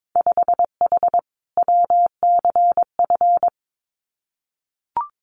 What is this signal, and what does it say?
Morse code, heard as a keyed tone, about 22 wpm, 700 Hz. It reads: 5H WCF
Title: Winter Field Day Exchanges Encore 22wpm